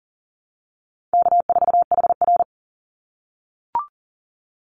Morse code, heard as a keyed tone, about 40 wpm, 700 Hz. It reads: K4HR
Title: Call Signs - 40wpm